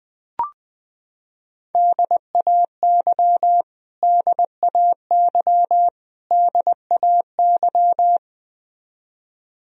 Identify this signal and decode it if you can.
Morse code, heard as a keyed tone, about 21 wpm, 700 Hz. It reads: DAY DAY DAY
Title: Top 100 Words Repeated 20wpm